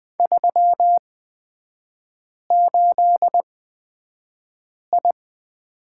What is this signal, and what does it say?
Morse code, heard as a keyed tone, about 20 wpm, spaced as for 5 wpm, 700 Hz. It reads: 38I
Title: 224 Lesson 36e - BT - Three Characters 20wpm